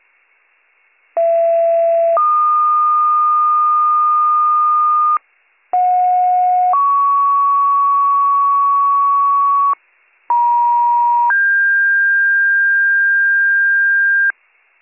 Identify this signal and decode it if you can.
Two-tone page, heard as a one-second tone, then a three-second tone, about 670 then 1220 Hz, about 730 then 1120 Hz, about 950 then 1600 Hz. It is heard from 1.2 to 14.3 s.